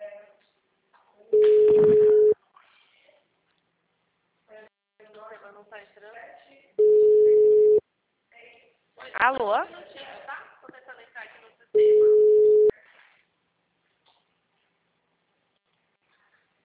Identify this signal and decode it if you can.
Ringback tone (Europe, 425 Hz), with speech.